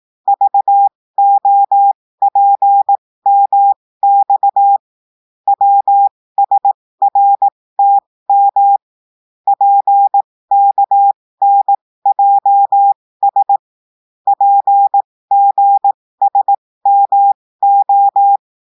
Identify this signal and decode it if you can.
Morse code, heard as a keyed tone, about 18 wpm, spaced as for 15 wpm, 800 Hz. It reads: VOPMX WSRTM PKNJS PGSMO
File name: cours-du-lundi files / cours-1580-texte-chiffre-15-mpm